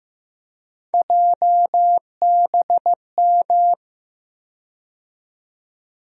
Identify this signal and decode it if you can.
Morse code, heard as a keyed tone, about 15 wpm, 700 Hz. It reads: JBM
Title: USA Call Sign Postfixes - 3 Letters 15wpm